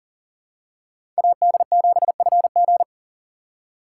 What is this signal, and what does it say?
Morse code, heard as a keyed tone, about 40 wpm, 700 Hz. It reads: AD7FG
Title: USA Call Signs - 5 Characters - 6x Character Spacing - 40wpm